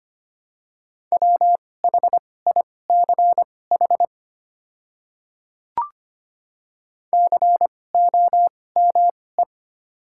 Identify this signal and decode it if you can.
Morse code, heard as a keyed tone, about 25 wpm, spaced as for 18 wpm, 690 Hz. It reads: WHICH COME
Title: 2x Character Spacing - Top 100 Words - 25wpm